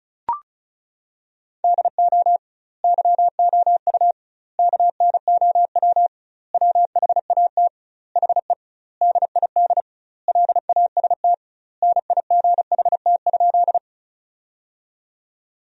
Morse code, heard as a keyed tone, about 36 wpm, 700 Hz. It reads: DO YOU KNOW WHAT HE DID LAST NIGHT?
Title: 2x Word-Spacing - Sentences from Top 300 Words - 35wpm